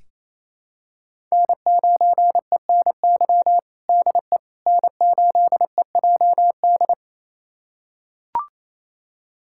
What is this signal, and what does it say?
Morse code, heard as a keyed tone, about 28 wpm, 700 Hz. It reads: N9ENY DE N8EJD
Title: USA Callsign DE Callsign - 5 Characters - 28wpm